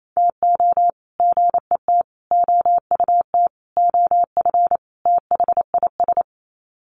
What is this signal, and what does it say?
Morse code, heard as a keyed tone, about 28 wpm, 700 Hz. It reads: TO GET OUT OF THIS
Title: Sets of 5 Words Encore2 28wpm